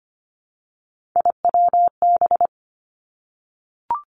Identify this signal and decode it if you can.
Morse code, heard as a keyed tone, about 25 wpm, 700 Hz. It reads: IWB
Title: Three Challenging Letter-Number Combinations 25wpm